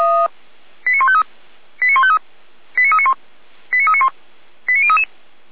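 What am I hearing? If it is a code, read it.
Five-tone selective call (ZVEI), heard as quick runs of five tones, tones of about 70 ms. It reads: 89153 89153 89391 89391 89033
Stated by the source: Received on SDR as ZVEI five-tone selective call 89207.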